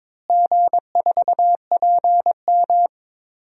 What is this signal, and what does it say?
Morse code, heard as a keyed tone, about 22 wpm, 700 Hz. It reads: G4PM
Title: DXCC-100 Call Signs - 4 Characters - Repeated 3x - 22wpm